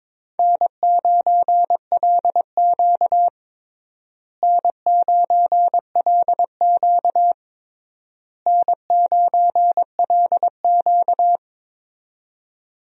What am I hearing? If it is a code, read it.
Morse code, heard as a keyed tone, about 22 wpm, 700 Hz. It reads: N9LQ N9LQ N9LQ
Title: USA Call Signs - 4 Characters - Repeated 3x - 22wpm